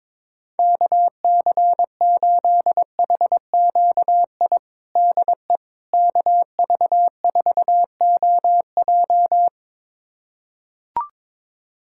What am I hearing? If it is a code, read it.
Morse code, heard as a keyed tone, about 21 wpm, 700 Hz. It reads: KC8HQI DE KV4OJ
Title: USA Callsign DE Callsign 22wpm